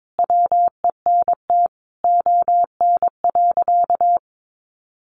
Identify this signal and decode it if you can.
Morse code, heard as a keyed tone, about 22 wpm, 700 Hz. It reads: WENT ON.
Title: Sentences from Top 200 Words 22wpm